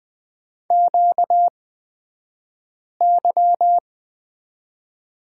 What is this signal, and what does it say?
Morse code, heard as a keyed tone, about 20 wpm, spaced as for 5 wpm, 700 Hz. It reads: QY
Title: 235 Lesson 38d - Letter q - Two Characters 20wpm